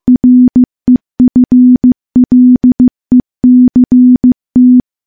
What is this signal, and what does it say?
Morse code, heard as a keyed tone, about 16 wpm, 250 Hz. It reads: REFLECT
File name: morse-training-15WPM